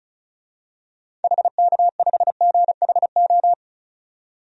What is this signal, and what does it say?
Morse code, heard as a keyed tone, about 36 wpm, 700 Hz. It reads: HK5GHO
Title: DXCC-100 Call Signs - 6 Characters - 35wpm